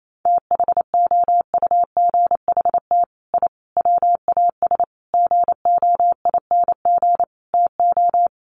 Morse code, heard as a keyed tone, about 28 wpm, 700 Hz. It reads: THOUGHT I WAS GOING TO
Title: Sets of 5 Words Encore2 28wpm